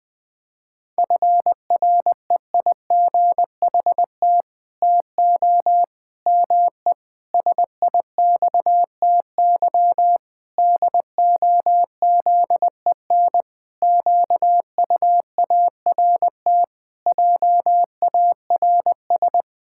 Morse code, heard as a keyed tone, about 20 wpm, 700 Hz. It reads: FREIGHT TO ME SIXTY DOZEN QUART JARS
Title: Pangram Challenge v2 - 20wpm